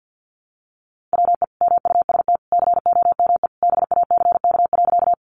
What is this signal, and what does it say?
Morse code, heard as a keyed tone, about 50 wpm, 700 Hz. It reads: WE MUST COME BACK.
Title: Sentences from Top 200 Words 50wpm